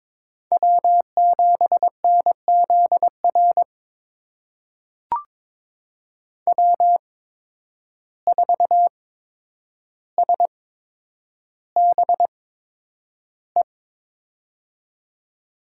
Morse code, heard as a keyed tone, about 22 wpm, spaced as for 9 wpm, 700 Hz. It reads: W7NZR W4SBE
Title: USA Call Signs - 5 Characters - 8x Character Spacing - 22wpm